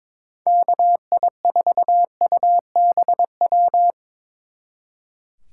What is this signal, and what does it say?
Morse code, heard as a keyed tone, about 24 wpm, 700 Hz. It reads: KI4UBW
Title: Speed Racing - USA 2x3 Call Signs 22wpm